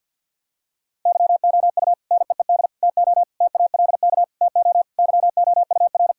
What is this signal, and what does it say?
Morse code, heard as a keyed tone, about 50 wpm, 700 Hz. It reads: YOU NEED TO TALK TO YOUR
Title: Sentences from Top 1600 Words 50wpm